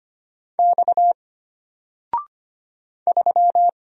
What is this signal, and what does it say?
Morse code, heard as a keyed tone, about 25 wpm, 700 Hz. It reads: X 3
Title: Single Letter-Number Alt 25wpm